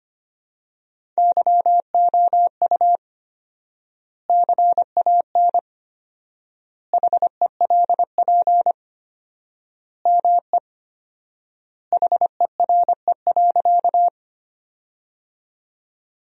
Morse code, heard as a keyed tone, about 25 wpm, 700 Hz. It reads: YOU CAN HELP ME HERE.
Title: 4x Word-Spacing - Sentences from Top 200 Words - 25wpm